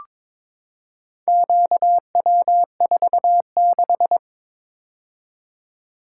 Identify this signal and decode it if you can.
Morse code, heard as a keyed tone, about 22 wpm, 700 Hz. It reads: QW46
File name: practice / 22wpm / Four Letter-Number Combinations - 22wpm